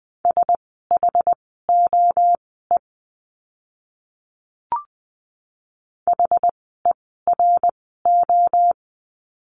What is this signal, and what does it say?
Morse code, heard as a keyed tone, about 20 wpm, spaced as for 14 wpm, 700 Hz. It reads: SHOE HERO
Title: 068 Lesson 12f - Letter d - Words 20wpm